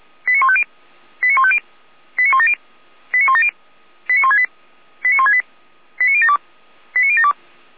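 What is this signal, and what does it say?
Five-tone selective call (ZVEI), heard as quick runs of five tones, tones of about 70 ms. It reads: 89150 89150 89170 89170 89168 89168 89072 89072
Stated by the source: ZVEI five-tone selective call received on SDR.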